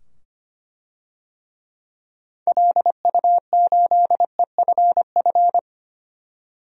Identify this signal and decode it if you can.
Morse code, heard as a keyed tone, about 25 wpm, 700 Hz. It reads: LU8EFF
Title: DXCC-100 Call Signs - 6 Characters - 25wpm